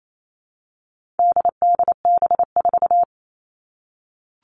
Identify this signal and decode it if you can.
Morse code, heard as a keyed tone, about 28 wpm, 700 Hz. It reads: DDB4
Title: All Cut Numbers - 4-Digit 28wpm